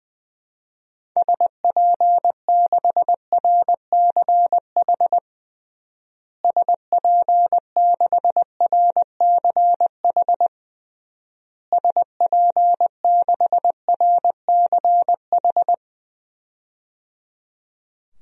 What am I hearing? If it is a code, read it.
Morse code, heard as a keyed tone, about 20 wpm, 700 Hz. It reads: SP6RCH SP6RCH SP6RCH
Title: DXCC-100 Call Signs - 6 Characters - Repeated 3x - 20wpm